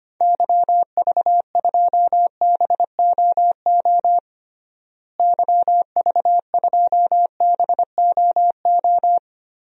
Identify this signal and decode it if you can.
Morse code, heard as a keyed tone, about 25 wpm, 700 Hz. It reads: YV2BOO YV2BOO
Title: DXCC-100 Call Signs - 6 Characters - Repeated 2x - 25wpm